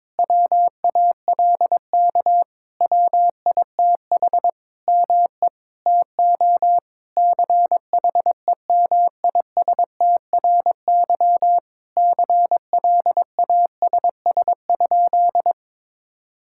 Morse code, heard as a keyed tone, about 22 wpm, 700 Hz. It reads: WALK WITH ME TO CHEMISTRY CLASS?
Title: Sentences from Top 1000 Words - Review Part 6 - 22wpm